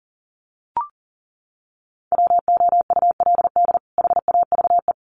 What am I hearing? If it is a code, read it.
Morse code, heard as a keyed tone, about 40 wpm, 700 Hz. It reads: WOULD HAVE
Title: Sets of 2 Words 40wpm